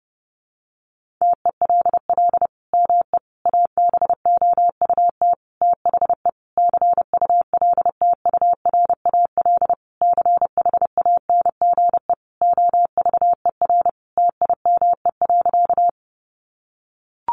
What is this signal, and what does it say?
Morse code, heard as a keyed tone, about 30 wpm, 700 Hz. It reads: TELL ME ABOUT THE CULTURAL CHANGE OVER TIME.
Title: Sentences from Top 1500 Words 30wpm